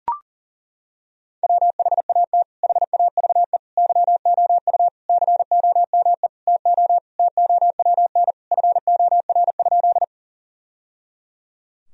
Morse code, heard as a keyed tone, about 40 wpm, 700 Hz. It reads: WHAT HAVE YOU COME TO TOWN FOR?
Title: Sentences from Top 1000 Words - Review Part 2 - 40wpm